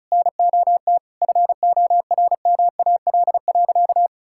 Morse code, heard as a keyed tone, about 36 wpm, 700 Hz. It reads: NOT FORMAL.
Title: Sentences from Top 1600 Words 35wpm